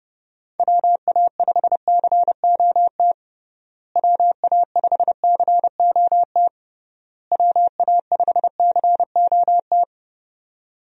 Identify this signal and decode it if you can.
Morse code, heard as a keyed tone, about 31 wpm, 700 Hz. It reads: WA5COT WA5COT WA5COT
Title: USA Call Signs - 6 Characters - Repeated 3x - 30wpm